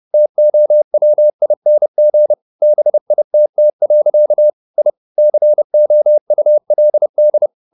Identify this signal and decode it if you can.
Morse code, heard as a keyed tone, about 30 wpm, 600 Hz. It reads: TOWING BITT. I COULD
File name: the old man and the sea / the old man and the sea 1 30wpm